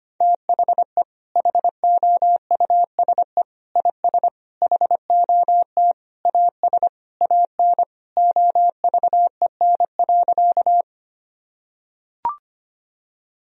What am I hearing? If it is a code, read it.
Morse code, heard as a keyed tone, about 25 wpm, 700 Hz. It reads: THE HOUSE IS HOT AS AN OVEN.
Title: Sentences from Top 1000 Words - Review Part 4 - 25wpm